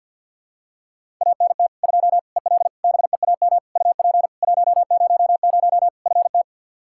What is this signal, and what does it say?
Morse code, heard as a keyed tone, about 51 wpm, 700 Hz. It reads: ANT 2 EL BEAM UP 100 FT
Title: K1USN SST QSO Practice 50wpm